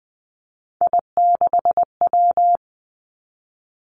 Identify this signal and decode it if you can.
Morse code, heard as a keyed tone, about 20 wpm, 700 Hz. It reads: I6W
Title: Three Challenging Letter-Number Combinations 20wpm